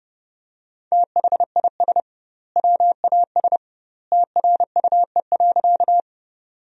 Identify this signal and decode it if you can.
Morse code, heard as a keyed tone, about 30 wpm, 700 Hz. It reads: THIS WAS TRUE.
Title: 2x Word-Spacing - Sentences from Top 400 Words - 30wpm